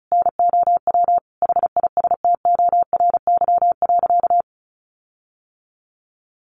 Morse code, heard as a keyed tone, about 34 wpm, 700 Hz. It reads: NOW HISTORY.
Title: Sentences from Top 1500 Words 35wpm